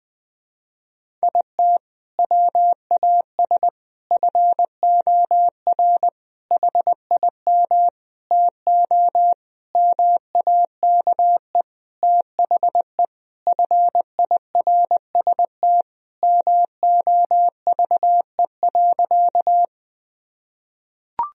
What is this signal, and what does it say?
Morse code, heard as a keyed tone, about 20 wpm, 700 Hz. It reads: IT WAS FOR HIM TO MAKE THE FIRST MOVE.
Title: Sentences from Top 1000 Words - Review Part 1 - 20wpm